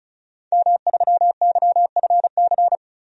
Morse code, heard as a keyed tone, about 35 wpm, 700 Hz. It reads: M3YFC
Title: DXCC-100 Call Signs - 5 Characters - 35wpm